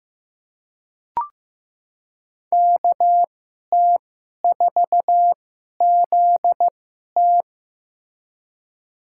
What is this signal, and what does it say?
Morse code, heard as a keyed tone, about 15 wpm, spaced as for 11 wpm, 700 Hz. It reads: KT4ZT
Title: USA Call Signs - 5 Characters - 2x Character Spacing - 15wpm